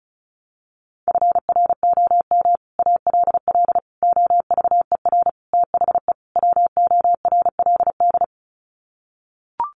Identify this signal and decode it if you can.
Morse code, heard as a keyed tone, about 34 wpm, 700 Hz. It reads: FROM ALL OVER THE WORLD
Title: Sets of 5 Words 35wpm